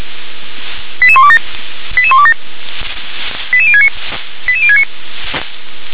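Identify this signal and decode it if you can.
Five-tone selective call (ZVEI), heard as quick runs of five tones, tones of about 70 ms. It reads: 88127 88127 88069 88069
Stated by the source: Received by SDR, ZVEI five-tone call 88069.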